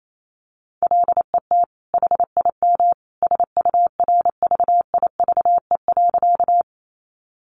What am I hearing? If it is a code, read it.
Morse code, heard as a keyed tone, about 28 wpm, 700 Hz. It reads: LET HIM SURVIVE.
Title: Sentences from Top 1300 Words 28wpm